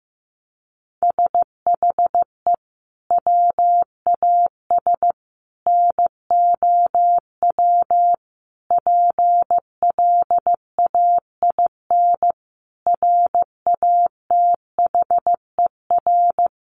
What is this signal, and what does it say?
Morse code, heard as a keyed tone, about 15 wpm, 700 Hz. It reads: SHE WAS NOW PLAIN RATHER
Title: Sentences from Top 1000 Words 15wpm